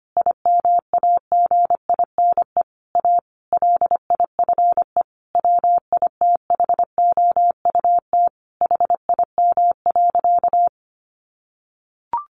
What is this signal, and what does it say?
Morse code, heard as a keyed tone, about 25 wpm, 700 Hz. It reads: IMAGINE A LIFE WITHOUT HIM.
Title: Sentences from Top 1400 Words 25wpm